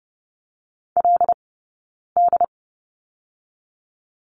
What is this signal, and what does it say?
Morse code, heard as a keyed tone, about 30 wpm, 700 Hz. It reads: L D
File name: course / 30:10wpm / 086 Lesson 15d - Number 5 - Two Characters 30wpm